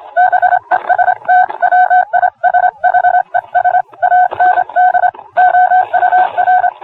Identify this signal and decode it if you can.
Morse code, heard as a keyed tone, about 26 wpm, 730 Hz. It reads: BEST WISHES AND OUR